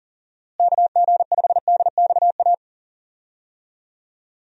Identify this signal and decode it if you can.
Morse code, heard as a keyed tone, about 38 wpm, 700 Hz. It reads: KG5DXA
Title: USA Call Signs - 6 Characters - Repeated 2x - 40wpm